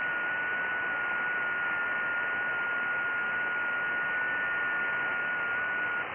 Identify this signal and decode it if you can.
FT8.
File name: IC-7100